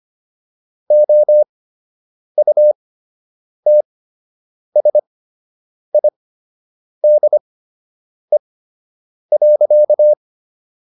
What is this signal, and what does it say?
Morse code, heard as a keyed tone, about 25 wpm, spaced as for 8 wpm, 600 Hz. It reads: OUTSIDE.